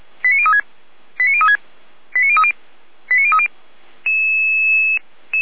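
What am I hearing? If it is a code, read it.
Five-tone selective call (ZVEI), heard as quick runs of five tones, tones of about 70 ms. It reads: 89036 89036 89030 89030
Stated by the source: ZVEI five-tone call received by SDR.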